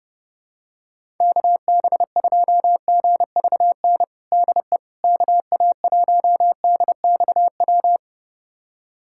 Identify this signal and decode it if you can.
Morse code, heard as a keyed tone, about 30 wpm, 700 Hz. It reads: KB2GVN DE KA1DXW
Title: USA Callsign DE Callsign - 6 Characters - 30wpm